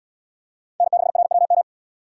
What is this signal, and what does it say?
Morse code, heard as a keyed tone, about 76 wpm, 700 Hz. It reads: N6RWM